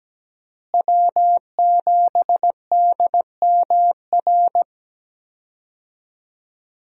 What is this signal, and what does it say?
Morse code, heard as a keyed tone, about 17 wpm, 700 Hz. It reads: W7DMR
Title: USA Call Signs - 5 Characters - Repeated 3x - 17wpm